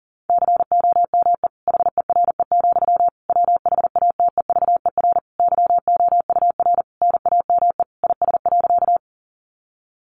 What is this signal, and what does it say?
Morse code, heard as a keyed tone, about 40 wpm, 700 Hz. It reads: COME HERE, WHATEVER YOUR NAME IS.
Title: Sentences from Top 1100 Words 40wpm